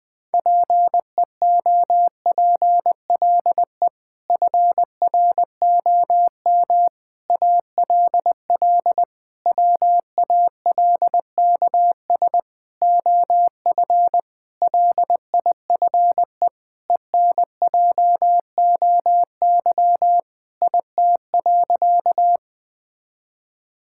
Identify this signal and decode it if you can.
Morse code, heard as a keyed tone, about 20 wpm, 700 Hz. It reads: PEOPLE FROM ALL WALKS OF LIFE ENJOY IT.
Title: Sentences with 6-Word Phrases - Spoken Beginning - 20wpm